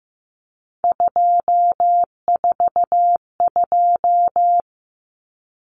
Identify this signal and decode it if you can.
Morse code, heard as a keyed tone, about 15 wpm, 700 Hz. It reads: 242
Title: Numbers - 3-Digit 15wpm